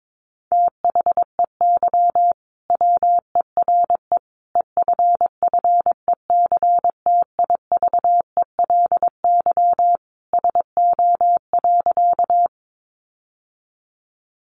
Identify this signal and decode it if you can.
Morse code, heard as a keyed tone, about 22 wpm, 700 Hz. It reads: THEY WERE EFFECTIVELY SO.
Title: Sentences from Top 1400 Words 22wpm